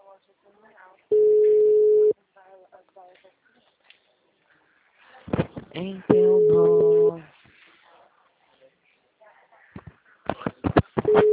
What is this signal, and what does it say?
Ringback tone (Europe, 425 Hz), with speech.